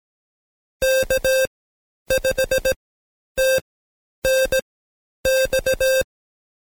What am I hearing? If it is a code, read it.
Morse code, heard as a keyed tone, about 18 wpm, spaced as for 10 wpm, 530 Hz. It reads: K5TNX
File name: Code Quick / AudioDownld2 / Seg131-Stepper4 9WPM